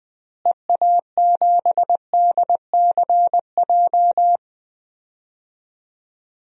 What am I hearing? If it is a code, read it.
Morse code, heard as a keyed tone, about 20 wpm, 700 Hz. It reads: EA7DCJ
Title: DXCC-100 Call Signs - 6 Characters - Repeated 2x - 20wpm